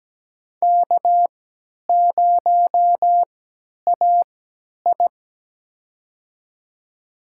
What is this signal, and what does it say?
Morse code, heard as a keyed tone, about 17 wpm, spaced as for 10 wpm, 700 Hz. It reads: K0AI
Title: USA Call Signs - 4 Characters - 3x Character Spacing - 17wpm